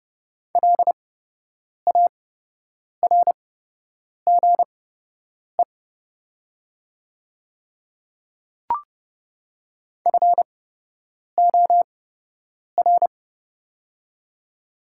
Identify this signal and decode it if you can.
Morse code, heard as a keyed tone, about 30 wpm, spaced as for 8 wpm, 700 Hz. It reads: LARGE FOR
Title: 8x Character Spacing - Top 300 Words - 30wpm